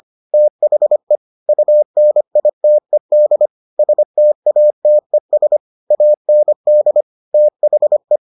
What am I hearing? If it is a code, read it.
Morse code, heard as a keyed tone, about 25 wpm, 600 Hz. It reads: THE UNITED STATES AND THE